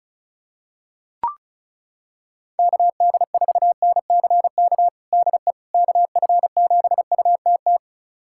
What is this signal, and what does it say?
Morse code, heard as a keyed tone, about 36 wpm, 700 Hz. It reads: KD4NCK DE KF7UTT
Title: USA Callsign DE Callsign 35wpm